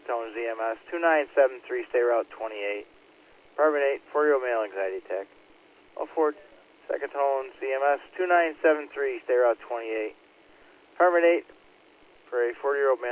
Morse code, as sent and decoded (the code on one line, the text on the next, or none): none